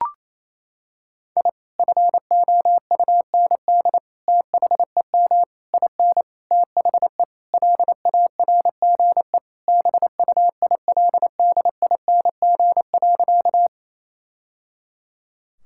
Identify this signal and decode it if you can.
Morse code, heard as a keyed tone, about 28 wpm, 700 Hz. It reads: I FOUND THEM IN THE LARGE BUILDING.
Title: Sentences from Top 1000 Words - Review Part 5 - 28wpm